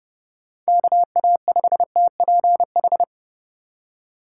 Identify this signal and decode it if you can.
Morse code, heard as a keyed tone, about 32 wpm, 700 Hz. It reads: KA5TPH
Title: USA Call Signs - 6 Characters - 4x Character Spacing - 30wpm